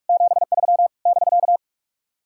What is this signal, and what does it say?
Morse code, heard as a keyed tone, about 60 wpm, spaced as for 53 wpm, 700 Hz. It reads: 73 <BK>
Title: Speed-Racing Common QSO Phrases Encore - 45wpm